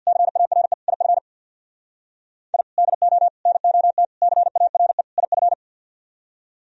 Morse code, heard as a keyed tone, about 64 wpm, 700 Hz. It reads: CARE IF I DO NOT CARE IF